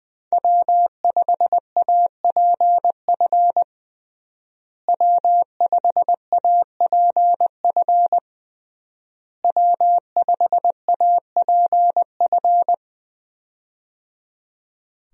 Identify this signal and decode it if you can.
Morse code, heard as a keyed tone, about 20 wpm, 700 Hz. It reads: W5APF W5APF W5APF
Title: USA Call Signs - 5 Characters - Repeated 3x - 20wpm